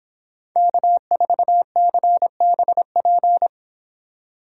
Morse code, heard as a keyed tone, about 27 wpm, 700 Hz. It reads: K4CBP
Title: Speed Racing - USA 1x3 Call Signs 22wpm